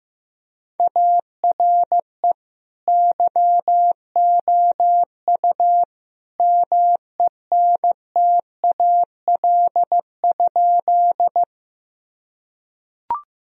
Morse code, heard as a keyed tone, about 15 wpm, 700 Hz. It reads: ARE YOU MENTAL?
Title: Sentences from Top 1000 Words - Review Part 4 - 15wpm